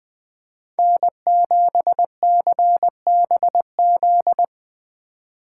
Morse code, heard as a keyed tone, about 20 wpm, 700 Hz. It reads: N7CBZ
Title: Speed Racing - USA Call Signs 17wpm